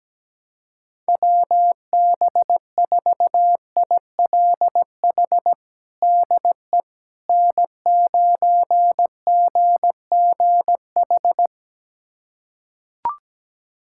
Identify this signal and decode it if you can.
Morse code, heard as a keyed tone, about 17 wpm, 700 Hz. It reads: WB4ILH DE N9GGH
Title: USA Callsign DE Callsign 17wpm